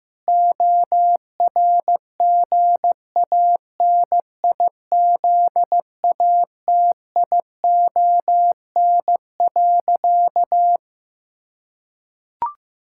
Morse code, heard as a keyed tone, about 15 wpm, 700 Hz. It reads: ORGANIZATION.
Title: Sentences from Top 1000 Words - Review Part 6 - 15wpm